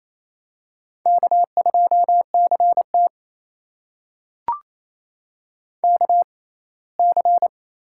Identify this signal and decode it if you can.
Morse code, heard as a keyed tone, about 28 wpm, spaced as for 19 wpm, 700 Hz. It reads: K2CT KC
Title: USA Call Signs - 6x Character Spacing - 28wpm